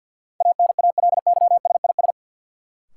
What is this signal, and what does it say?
Morse code, heard as a keyed tone, about 53 wpm, 700 Hz. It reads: ANALYSIS
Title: Sentences from Top 1400 Words - New Word Drill Alt - 50wpm